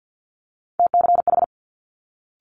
Morse code, heard as a keyed tone, about 50 wpm, 700 Hz. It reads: TCH
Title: Multi-letter Phonemes Quick Fire 50wpm